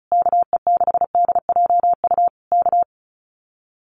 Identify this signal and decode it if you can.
Morse code, heard as a keyed tone, about 35 wpm, 700 Hz. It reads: KE6DJU K
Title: Common QSO Phrases Encore 35wpm